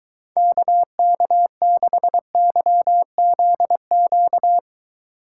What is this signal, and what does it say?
Morse code, heard as a keyed tone, about 23 wpm, 700 Hz. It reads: KK6YZQ